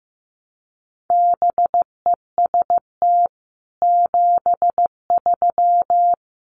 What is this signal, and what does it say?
Morse code, heard as a keyed tone, about 15 wpm, 700 Hz. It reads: BEST 73
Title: Common QSO Phrases 15wpm